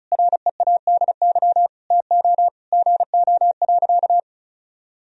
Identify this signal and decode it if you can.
Morse code, heard as a keyed tone, about 34 wpm, 700 Hz. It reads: READY TO GO.
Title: Sentences from Top 1000 Words - Review Part 2 - 35wpm